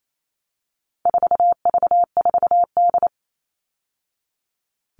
Morse code, heard as a keyed tone, about 28 wpm, 700 Hz. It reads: 4V4D
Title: All Cut Numbers - 4-Digit 28wpm